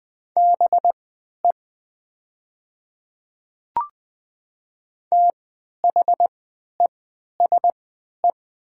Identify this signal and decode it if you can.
Morse code, heard as a keyed tone, about 20 wpm, spaced as for 11 wpm, 700 Hz. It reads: BE THESE
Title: 3x Character Spacing - Top 300 Words - 20wpm